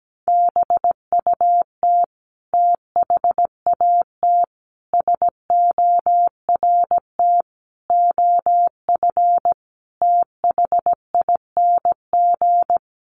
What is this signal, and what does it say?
Morse code, heard as a keyed tone, about 17 wpm, 700 Hz. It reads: BUT THAT SORT OF THING
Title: Sentences from Top 1300 Words 17wpm